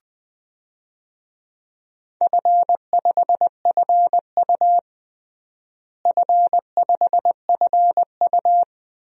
Morse code, heard as a keyed tone, about 20 wpm, 700 Hz. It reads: F5FU F5FU
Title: DXCC-100 Call Signs - 4 Characters - Repeated 2x - 20wpm